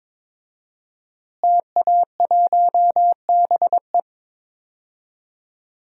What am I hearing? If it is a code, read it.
Morse code, heard as a keyed tone, about 22 wpm, 700 Hz. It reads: TA1BE
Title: DXCC-100 Call Signs - 5 Characters - Repeated 3x - 22wpm